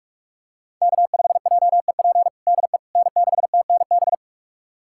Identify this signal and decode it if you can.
Morse code, heard as a keyed tone, about 41 wpm, 700 Hz. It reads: K5JEP DE N6TNB
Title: USA Callsign DE Callsign - 5 Characters - 45wpm